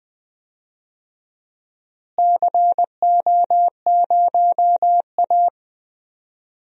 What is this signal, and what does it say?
Morse code, heard as a keyed tone, about 20 wpm, 700 Hz. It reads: CO0A
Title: DXCC-100 Call Signs - 4 Characters - Repeated 2x - 20wpm